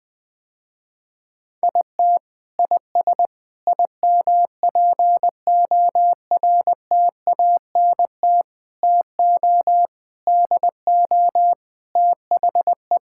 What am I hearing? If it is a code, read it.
Morse code, heard as a keyed tone, about 20 wpm, 700 Hz. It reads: IT IS IMPORTANT TO DO THE
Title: Sentences with 5-Word Phrases - Spoken Beginning - 20wpm